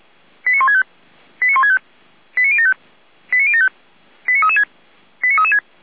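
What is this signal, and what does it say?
Five-tone selective call (ZVEI), heard as quick runs of five tones, tones of about 70 ms. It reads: 89165 89165 89075 89075 89336 89336
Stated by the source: ZVEI five-tone call received on SDR.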